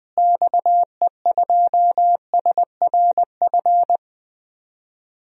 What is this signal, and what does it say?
Morse code, heard as a keyed tone, about 20 wpm, 700 Hz. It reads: XE2SRF
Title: DXCC-100 Call Signs - 6 Characters - Repeated 3x - 20wpm